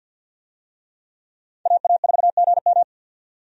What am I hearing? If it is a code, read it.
Morse code, heard as a keyed tone, about 53 wpm, 700 Hz. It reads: AA4GM
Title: USA Call Signs - 5 Characters - 50wpm